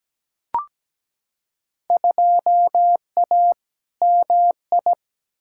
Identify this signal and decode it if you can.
Morse code, heard as a keyed tone, about 17 wpm, 700 Hz. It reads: 2A MI
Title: Field Day Exchanges - 17wpm